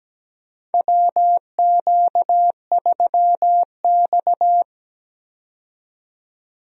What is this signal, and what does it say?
Morse code, heard as a keyed tone, about 17 wpm, 700 Hz. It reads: WQ3X
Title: USA Call Signs - 4 Characters - 17wpm